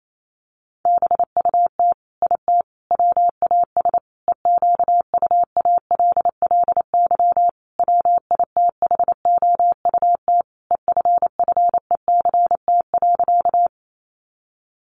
Morse code, heard as a keyed tone, about 28 wpm, 700 Hz. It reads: BUT IT WAS EQUALLY WITHOUT EFFECT.
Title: Sentences from Top 1400 Words 28wpm